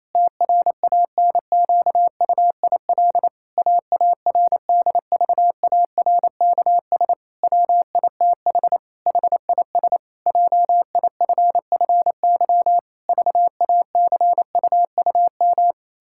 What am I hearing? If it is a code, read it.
Morse code, heard as a keyed tone, about 27 wpm, 700 Hz. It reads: TRANQUIL AARDVARKS WITH HIS JIFFY VACUUM
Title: Pangram Challenge v2 - 28wpm